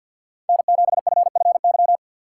Morse code, heard as a keyed tone, about 52 wpm, 690 Hz. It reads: N7FFY
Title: USA Call Signs - 5 Characters - 2x Character Spacing - 50wpm